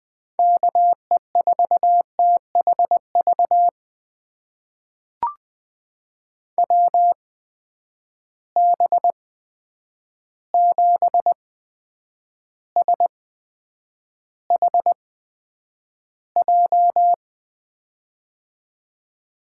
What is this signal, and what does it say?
Morse code, heard as a keyed tone, about 20 wpm, spaced as for 9 wpm, 700 Hz. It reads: KE4THV WB7SHJ
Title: USA Call Signs - 6 Characters - 8x Character Spacing - 20wpm